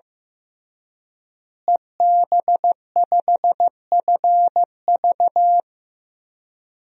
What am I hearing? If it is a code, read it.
Morse code, heard as a keyed tone, about 16 wpm, 700 Hz. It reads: EB5FV
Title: DXCC-100 Call Signs - 5 Characters - Repeated 3x - 15wpm